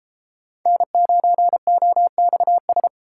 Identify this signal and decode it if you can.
Morse code, heard as a keyed tone, about 29 wpm, 700 Hz. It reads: N9OXS